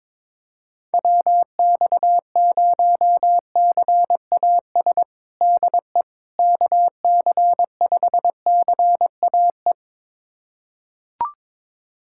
Morse code, heard as a keyed tone, about 22 wpm, 700 Hz. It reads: WX0CAS DE KC5CAE
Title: USA Callsign DE Callsign - 6 Characters - 22wpm